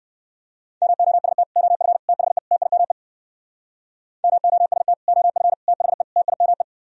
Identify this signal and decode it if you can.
Morse code, heard as a keyed tone, about 65 wpm, 700 Hz. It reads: MOST OF THE TIME MOST OF THE TIME